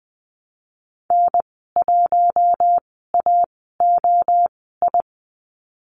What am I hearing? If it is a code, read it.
Morse code, heard as a keyed tone, about 20 wpm, spaced as for 15 wpm, 700 Hz. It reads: N1AOI
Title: 055 Lesson 10g - Letter r - Call Signs 20wpm